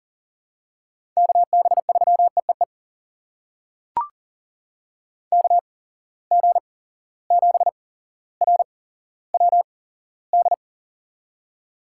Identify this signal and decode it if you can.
Morse code, heard as a keyed tone, about 40 wpm, spaced as for 17 wpm, 700 Hz. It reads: KB3EEE KG7RWD
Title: USA Call Signs - 6 Characters - 8x Character Spacing - 40wpm